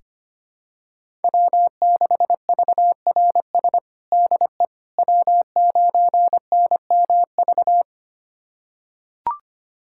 Morse code, heard as a keyed tone, about 25 wpm, 700 Hz. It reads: W6VRS DE W9NMV
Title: USA Callsign DE Callsign - 5 Characters - 25wpm